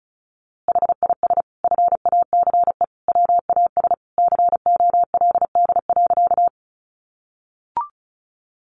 Morse code, heard as a keyed tone, about 34 wpm, 700 Hz. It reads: HIS FACE WAS COLD.
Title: A Sound Of Thunder 35wpm